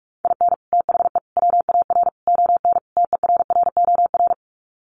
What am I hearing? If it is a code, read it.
Morse code, heard as a keyed tone, about 44 wpm, 690 Hz. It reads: IN THE WAR ON TERROR
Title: Sets of 5 Words Encore2 45wpm